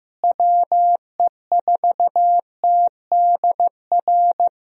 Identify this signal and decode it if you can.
Morse code, heard as a keyed tone, about 15 wpm, 700 Hz. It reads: WE4TDR
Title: USA Call Signs - 6 Characters - Repeated 3x - 15wpm